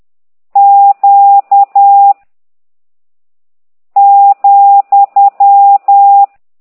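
Morse code, heard as a keyed tone, about 10 wpm, 800 Hz. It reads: Q ,
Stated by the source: Morse code lesson 5 exercise 10 by CT2IDW, containing only characters M, O, G, Z, Q and comma.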